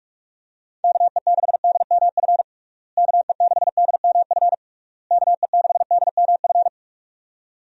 Morse code, heard as a keyed tone, about 46 wpm, 700 Hz. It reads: KE6DMF KE6DMF KE6DMF
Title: USA Call Signs - 6 Characters - Repeated 3x - 45wpm